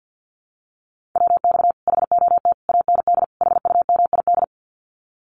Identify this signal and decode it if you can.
Morse code, heard as a keyed tone, about 50 wpm, 700 Hz. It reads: WX HOT AND HUMID
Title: Common QSO Phrases Encore 50wpm